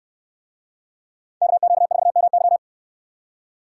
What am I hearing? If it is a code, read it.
Morse code, heard as a keyed tone, about 69 wpm, 700 Hz. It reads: KQ4MY